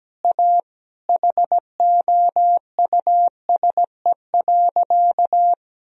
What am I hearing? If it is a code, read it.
Morse code, heard as a keyed tone, about 17 wpm, 700 Hz. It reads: A HOUSE.
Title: Sentences from Top 1000 Words - Review Part 1 - 17wpm